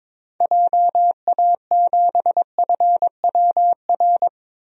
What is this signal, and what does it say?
Morse code, heard as a keyed tone, about 21 wpm, 700 Hz. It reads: JA7FWR
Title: DXCC-100 Call Signs - 6 Characters - 22wpm